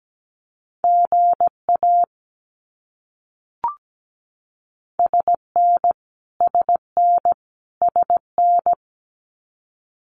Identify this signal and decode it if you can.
Morse code, heard as a keyed tone, about 17 wpm, 700 Hz. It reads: GA SN SN SN
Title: Abbreviations Repeated 3x - 17wpm